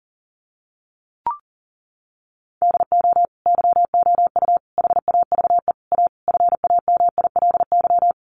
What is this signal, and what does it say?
Morse code, heard as a keyed tone, about 40 wpm, 700 Hz. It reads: DO YOU HAVE A FAMILY
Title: Sentences from Top 1500 Words 40wpm